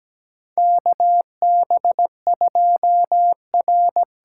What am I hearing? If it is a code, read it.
Morse code, heard as a keyed tone, about 17 wpm, 700 Hz. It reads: KB2R
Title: USA Call Signs - 8x Character Spacing - 17wpm